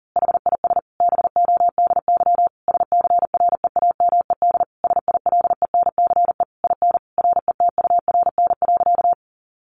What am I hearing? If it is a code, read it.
Morse code, heard as a keyed tone, about 40 wpm, 700 Hz. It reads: HIS BODY SCREAMED SILENCE IN RETURN.
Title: A Sound Of Thunder 40wpm